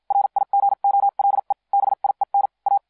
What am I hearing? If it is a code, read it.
Morse code, heard as a keyed tone, about 56 wpm, spaced as for 45 wpm, 800 Hz. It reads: RIGOLE BIEN A